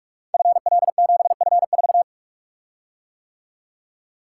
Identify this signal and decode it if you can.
Morse code, heard as a keyed tone, about 44 wpm, 700 Hz. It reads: FL7F4
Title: Code Groups - 5 Letter:Numbers - 45wpm